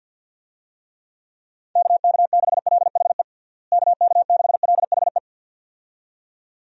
Morse code, heard as a keyed tone, about 51 wpm, 700 Hz. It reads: KK6LHE KK6LHE
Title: USA Call Signs - 6 Characters - Repeated 2x - 50wpm